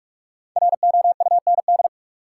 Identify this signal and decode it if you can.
Morse code, heard as a keyed tone, about 44 wpm, 700 Hz. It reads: ROUND
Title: Sets of 5 Words Encore - New Word Drill - 45wpm